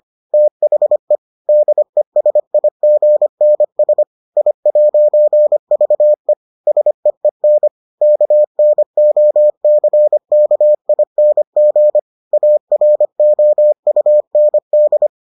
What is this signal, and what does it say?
Morse code, heard as a keyed tone, about 25 wpm, 600 Hz. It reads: THE DESIGNS I'VE SEEN KNOCKING AROUND